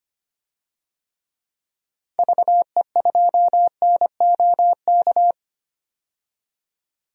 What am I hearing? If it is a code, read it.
Morse code, heard as a keyed tone, about 25 wpm, 700 Hz. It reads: VE2NOK
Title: DXCC-100 Call Signs - 6 Characters - 25wpm